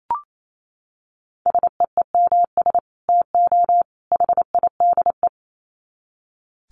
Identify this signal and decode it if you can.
Morse code, heard as a keyed tone, about 28 wpm, 700 Hz. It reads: SEEMS TO HIDE
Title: Dit Challenge 28wpm